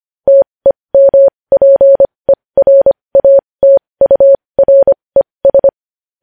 Morse code, heard as a keyed tone, about 25 wpm, spaced as for 20 wpm, 550 Hz. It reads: TEMPERATURES
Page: news-headlines-in-morse-code-at-20-wpm